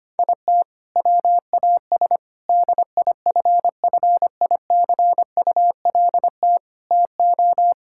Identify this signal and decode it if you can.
Morse code, heard as a keyed tone, about 25 wpm, 700 Hz. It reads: IT WAS DIFFICULT TO
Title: Sentences from Top 1000 Words - Review Part 6 - 25wpm